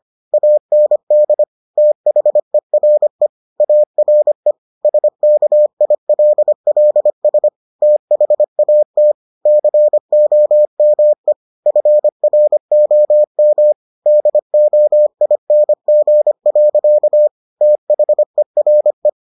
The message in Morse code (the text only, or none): AND THERE ARE SKILLS THAT COME FROM DOING. THERE